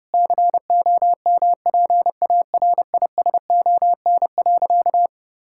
Morse code, heard as a keyed tone, about 32 wpm, 700 Hz. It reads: COMPARISON.
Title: Sentences from Top 1600 Words 30wpm